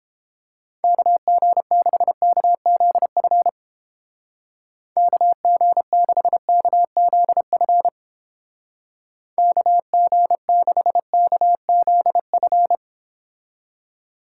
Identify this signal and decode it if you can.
Morse code, heard as a keyed tone, about 29 wpm, 700 Hz. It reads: KG6KZF KG6KZF KG6KZF